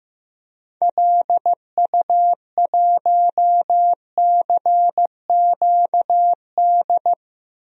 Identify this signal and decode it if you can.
Morse code, heard as a keyed tone, about 15 wpm, 700 Hz. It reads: LU1CQD
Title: DXCC-100 Call Signs - 6 Characters - 15wpm